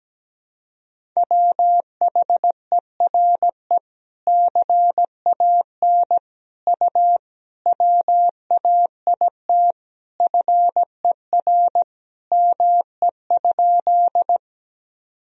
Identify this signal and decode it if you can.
Morse code, heard as a keyed tone, about 17 wpm, 700 Hz. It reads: WHERE CAN U WAIT FER ME?
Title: Sentences from Top 500 Words and Abbreviations 17wpm